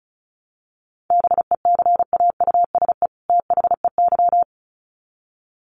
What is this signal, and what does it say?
Morse code, heard as a keyed tone, about 34 wpm, 700 Hz. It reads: BECAUSE THEY
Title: Sets of 2 Words 35wpm